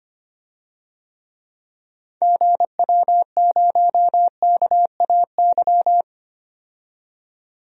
Morse code, heard as a keyed tone, about 25 wpm, 700 Hz. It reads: GW0KAY
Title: DXCC-100 Call Signs - 6 Characters - 25wpm